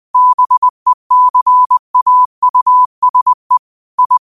Morse code, heard as a keyed tone, about 20 wpm, 1000 Hz. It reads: BECAUSE I
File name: 20wpm